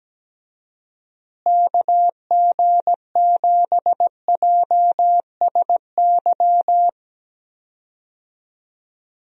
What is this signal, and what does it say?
Morse code, heard as a keyed tone, about 17 wpm, 700 Hz. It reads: KG7JSY
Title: USA Call Signs - 6 Characters - Repeated 2x - 17wpm